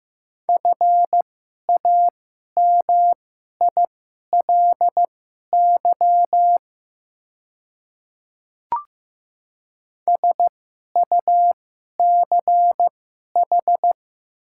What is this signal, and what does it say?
Morse code, heard as a keyed tone, about 15 wpm, spaced as for 11 wpm, 700 Hz. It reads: FAMILY SUCH
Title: 2x Character Spacing - Top 500 Words - 15wpm